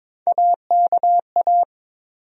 Morse code, heard as a keyed tone, about 22 wpm, 700 Hz. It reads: AKA
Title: USA Call Sign Postfixes - 3 Letters 22wpm